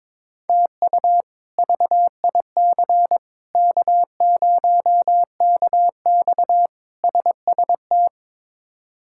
Morse code, heard as a keyed tone, about 22 wpm, 700 Hz. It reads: TU VIC K0KX SST
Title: K1USN SST QSO Practice - 22wpm